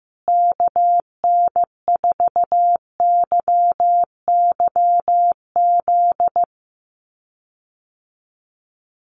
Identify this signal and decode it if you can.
Morse code, heard as a keyed tone, about 15 wpm, 700 Hz. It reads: KN4YYZ
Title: USA Call Signs Repeated 3x 15wpm